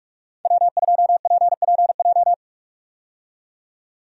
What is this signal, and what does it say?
Morse code, heard as a keyed tone, about 47 wpm, 700 Hz. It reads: W2PPJ
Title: USA Call Signs - 5 Characters - 45wpm